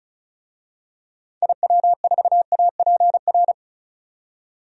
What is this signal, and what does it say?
Morse code, heard as a keyed tone, about 36 wpm, 700 Hz. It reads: IW4APR
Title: DXCC-100 Call Signs - 6 Characters - 35wpm